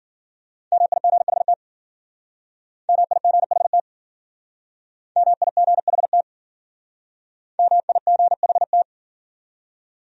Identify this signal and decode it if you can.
Morse code, heard as a keyed tone, about 50 wpm, 700 Hz. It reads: MIGHT MIGHT MIGHT MIGHT